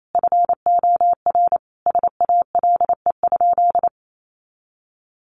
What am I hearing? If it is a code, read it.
Morse code, heard as a keyed tone, about 28 wpm, 700 Hz. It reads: FOR SALE?
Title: Sentences from Top 1500 Words 28wpm